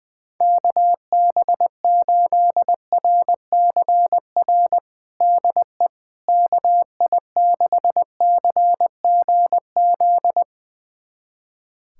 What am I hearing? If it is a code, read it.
Morse code, heard as a keyed tone, about 21 wpm, 700 Hz. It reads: KB8RCR DE KI6CGZ
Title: USA Callsign DE Callsign - 6 Characters - 20wpm